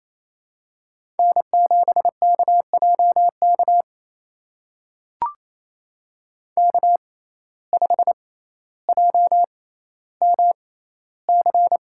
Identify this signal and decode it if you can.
Morse code, heard as a keyed tone, about 28 wpm, spaced as for 14 wpm, 700 Hz. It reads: N7KJK K5JMC
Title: USA Call Signs - 5 Characters - 6x Character Spacing - 28wpm